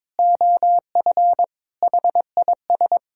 Morse code, heard as a keyed tone, about 22 wpm, 700 Hz. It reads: OF HIS
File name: practice / 22wpm / Sets of 8 Words - 22wpm